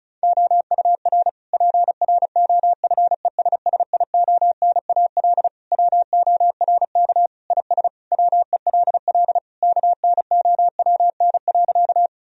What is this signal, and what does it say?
Morse code, heard as a keyed tone, about 36 wpm, 700 Hz. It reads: OUR PROFESSIONAL WORK IS WELL KNOWN.
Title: Sentences from Top 1000 Words - Review Part 4 - 35wpm